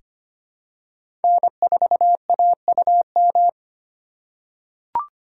Morse code, heard as a keyed tone, about 25 wpm, 690 Hz. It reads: N4AUM
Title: USA Call Signs - 4x Character Spacing - 25wpm